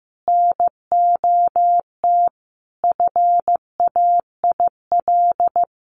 Morse code, heard as a keyed tone, about 15 wpm, 700 Hz. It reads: NOT FAIL
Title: Sentences from Top 1100 Words 15wpm